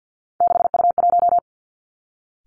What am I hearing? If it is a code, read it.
Morse code, heard as a keyed tone, about 50 wpm, 700 Hz. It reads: 6U1
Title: Three Challenging Letter-Number Combinations 50wpm